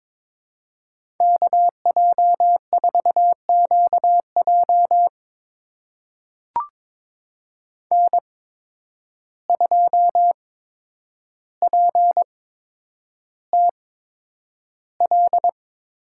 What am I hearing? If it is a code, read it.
Morse code, heard as a keyed tone, about 23 wpm, spaced as for 10 wpm, 700 Hz. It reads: KJ4QJ N2PTL
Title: USA Call Signs - 5 Characters - 8x Character Spacing - 22wpm